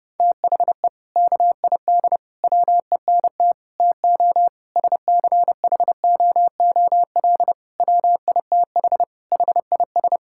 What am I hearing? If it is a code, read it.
Morse code, heard as a keyed tone, about 30 wpm, 700 Hz. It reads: THE KID WENT TO SCHOOL WITH HIS
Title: Sentences from Top 1600 Words 30wpm